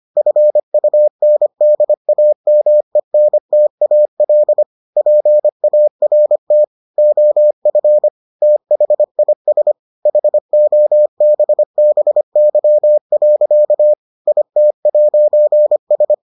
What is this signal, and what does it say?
Morse code, heard as a keyed tone, about 25 wpm, 600 Hz. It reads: FUNDAMENTAL PART OF THIS HOBBY. IT'S